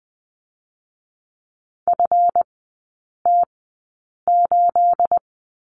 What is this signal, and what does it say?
Morse code, heard as a keyed tone, about 20 wpm, spaced as for 8 wpm, 690 Hz. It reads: FT8
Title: 212 Lesson 34e - Number 8 - Three Characters 20wpm